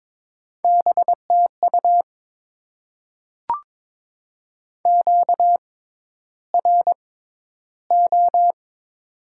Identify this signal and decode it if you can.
Morse code, heard as a keyed tone, about 22 wpm, spaced as for 11 wpm, 700 Hz. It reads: BTU QRO
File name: Top 90 QSO Elements - 6x Character Spacing - 22wpm